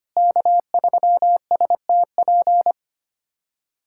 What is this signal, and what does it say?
Morse code, heard as a keyed tone, about 25 wpm, 700 Hz. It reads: K3STP
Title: USA Callsign DE Callsign - 5 Characters - 25wpm